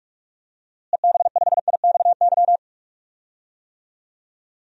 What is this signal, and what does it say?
Morse code, heard as a keyed tone, about 42 wpm, 700 Hz. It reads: EB5IXY
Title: DXCC-100 Call Signs - 6 Characters - Repeated 3x - 45wpm